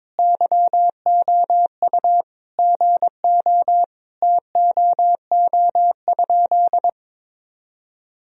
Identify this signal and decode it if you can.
Morse code, heard as a keyed tone, about 21 wpm, 700 Hz. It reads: YOU GO TOO?
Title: Sentences from Top 1000 Words - Review Part 2 - 22wpm